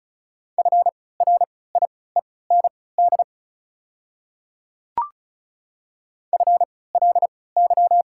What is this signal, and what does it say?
Morse code, heard as a keyed tone, about 34 wpm, spaced as for 20 wpm, 690 Hz. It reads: FRIEND FLY
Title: 3x Character Spacing - Top 500 Words - 35wpm